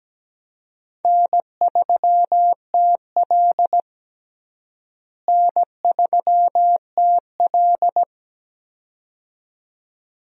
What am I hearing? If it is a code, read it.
Morse code, heard as a keyed tone, about 17 wpm, 700 Hz. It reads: N3TL N3TL
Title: USA Call Signs - 4 Characters - Repeated 2x - 17wpm